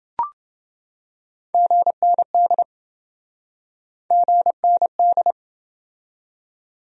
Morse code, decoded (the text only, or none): GND GND